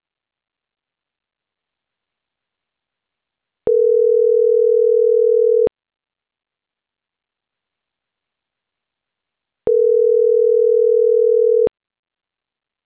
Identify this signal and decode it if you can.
Ringback tone (North America), 440 + 480 Hz.